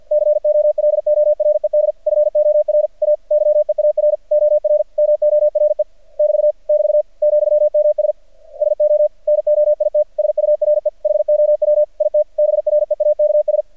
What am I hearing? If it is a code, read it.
Morse code, heard as a keyed tone, about 51 wpm, 600 Hz. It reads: COLORED FOR A YEAR OR MORE <BT> <BT> ,MS SO NOIT SURE HOW IT BREAKS
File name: archived / CRSnet